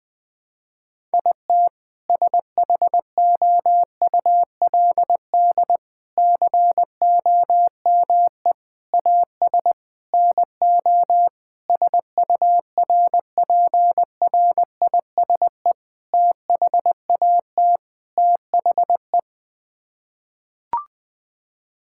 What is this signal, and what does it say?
Morse code, heard as a keyed tone, about 20 wpm, 700 Hz. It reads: IT SHOULD COME AS NO SURPRISE THAT THE
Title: Sets of 8 Words - 20wpm